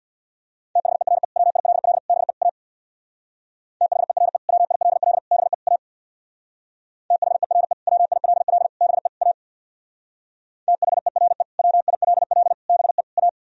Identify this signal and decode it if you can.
Morse code, heard as a keyed tone, about 64 wpm, 700 Hz. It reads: THERE WILL BE A THERE WILL BE A THERE WILL BE A THERE WILL BE A